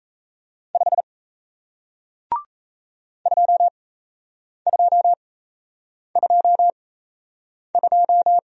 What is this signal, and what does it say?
Morse code, heard as a keyed tone, about 38 wpm, spaced as for 8 wpm, 700 Hz. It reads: 5 2222